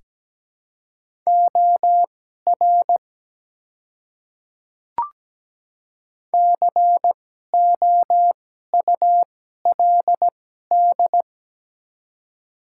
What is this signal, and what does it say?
Morse code, heard as a keyed tone, about 17 wpm, spaced as for 12 wpm, 690 Hz. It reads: OR COULD
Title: 2x Character Spacing - Top 100 Words - 17wpm